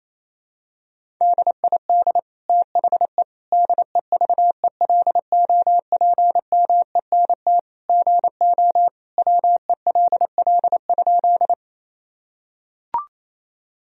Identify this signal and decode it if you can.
Morse code, heard as a keyed tone, about 28 wpm, 700 Hz. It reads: DID THE DEVELOPMENT GO WELL?
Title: Sentences from Top 1000 Words - Review Part 4 - 28wpm